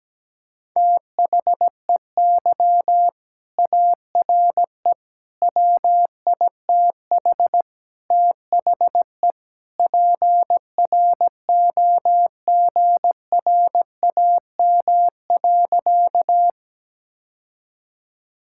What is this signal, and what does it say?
Morse code, heard as a keyed tone, about 17 wpm, 700 Hz. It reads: THEY ARE WITH THE PROGRAM.
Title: Sentences from Top 1000 Words - Review Part 5 - 17wpm